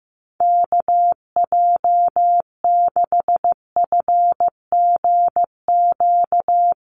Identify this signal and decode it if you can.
Morse code, heard as a keyed tone, about 15 wpm, 700 Hz. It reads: KJ6FGQ
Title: USA Call Signs 15wpm